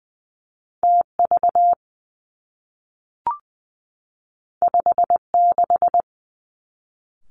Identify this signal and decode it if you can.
Morse code, heard as a keyed tone, about 19 wpm, 700 Hz. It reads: T V 5 6
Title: Two Challenging Letter-Number Combinations 20wpm